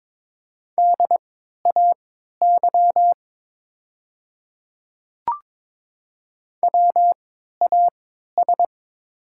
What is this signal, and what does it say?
Morse code, heard as a keyed tone, about 22 wpm, spaced as for 12 wpm, 700 Hz. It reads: DAY WAS
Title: 3x Character Spacing - Top 100 Words - 22wpm